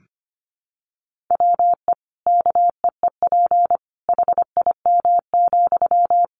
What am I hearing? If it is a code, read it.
Morse code, heard as a keyed tone, about 25 wpm, 700 Hz. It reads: WE KEEP HIM,
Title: Sentences from Top 600 Words 25wpm